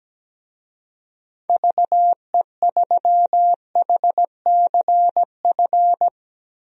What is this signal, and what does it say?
Morse code, heard as a keyed tone, about 17 wpm, 700 Hz. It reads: VE3HCF
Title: DXCC-100 Call Signs - 6 Characters - Repeated 3x - 17wpm